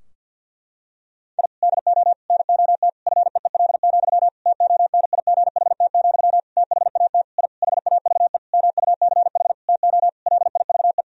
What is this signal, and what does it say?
Morse code, heard as a keyed tone, about 52 wpm, 700 Hz. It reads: I DO NOT FEEL, TONIGHT, THAT I HAVE MUCH TO LIVE
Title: Sentences from Top 1000 Words - Review Part 3 - 50wpm